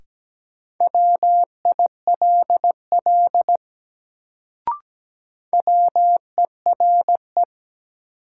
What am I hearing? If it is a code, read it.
Morse code, heard as a keyed tone, about 17 wpm, 700 Hz. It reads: WILL WERE
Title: Sentences from Top 100 Words - New Word Drill - 17wpm